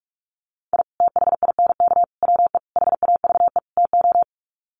Morse code, heard as a keyed tone, about 44 wpm, 700 Hz. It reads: I THINK WE HAVE TO
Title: Sets of 5 Words 45wpm